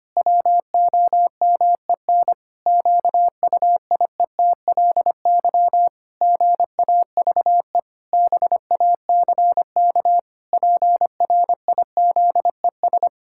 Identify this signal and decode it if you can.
Morse code, heard as a keyed tone, about 25 wpm, 700 Hz. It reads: WOMEN QUIETLY GAVE BACK PRIZES
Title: Pangram Challenge v2 - 25wpm